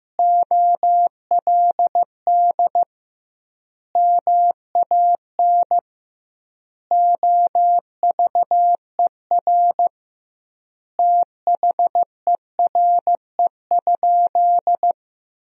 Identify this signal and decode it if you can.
Morse code, heard as a keyed tone, about 15 wpm, 700 Hz. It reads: OLD MAN OVER THERE?
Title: 2x Word-Spacing - Sentences from Top 200 Words - 15wpm